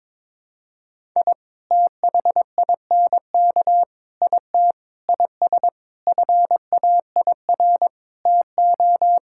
Morse code, heard as a keyed tone, about 22 wpm, 700 Hz. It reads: I THINK IT IS FAIR TO
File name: practice / 22wpm / Sets of 8 Words - 22wpm